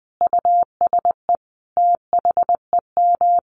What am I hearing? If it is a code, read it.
Morse code, heard as a keyed tone, about 20 wpm, 700 Hz. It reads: USE THEM
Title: Sentences from Top 1300 Words 20wpm